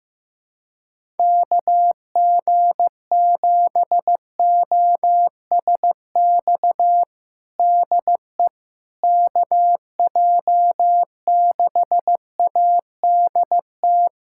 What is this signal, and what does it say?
Morse code, heard as a keyed tone, about 15 wpm, 700 Hz. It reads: KG7OSX DE KJ6ADT
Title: USA Callsign DE Callsign 15wpm